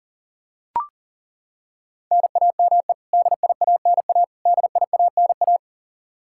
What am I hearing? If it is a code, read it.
Morse code, heard as a keyed tone, about 37 wpm, 700 Hz. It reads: NAME DIANA DIANA
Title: Practice QSO - Exchange 1 - Beginning - 40wpm